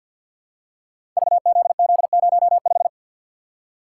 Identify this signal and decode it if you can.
Morse code, heard as a keyed tone, about 53 wpm, 700 Hz. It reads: VZZ05
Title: Code Groups - 5 Letter:Numbers - 50wpm